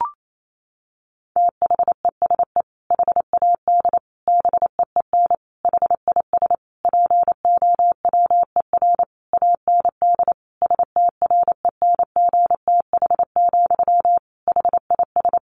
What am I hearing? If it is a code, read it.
Morse code, heard as a keyed tone, about 28 wpm, 700 Hz. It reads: THESE HAD BEEN HIS POWER AND STRENGTH, HIS
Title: Sentences from Top 1500 Words 28wpm